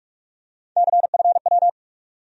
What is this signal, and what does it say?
Morse code, heard as a keyed tone, about 46 wpm, 700 Hz. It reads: CFW
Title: USA Call Sign Postfixes - 3 Letters 45wpm